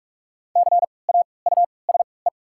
Morse code, heard as a keyed tone, about 44 wpm, 690 Hz. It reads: C A U S E
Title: 3x Character Spacing - Top 200 Words - 45wpm